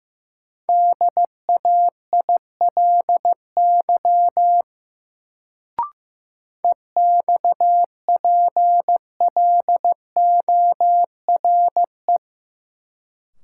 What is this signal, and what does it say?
Morse code, heard as a keyed tone, about 16 wpm, 700 Hz. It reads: DAILY EXPLORE
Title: Sentences from Top 1100 Words - New Word Drill - 15wpm